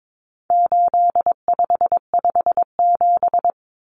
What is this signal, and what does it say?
Morse code, heard as a keyed tone, about 22 wpm, 700 Hz. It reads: 8557
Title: Numbers - 4-Digit 22wpm